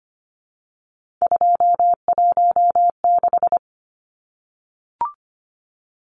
Morse code, heard as a keyed tone, about 25 wpm, 700 Hz. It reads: 216
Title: Numbers - 3-Digit 25wpm